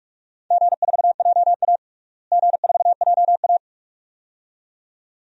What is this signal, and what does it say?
Morse code, heard as a keyed tone, about 46 wpm, 700 Hz. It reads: G4JA G4JA
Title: DXCC-100 Call Signs - 4 Characters - Repeated 2x - 45wpm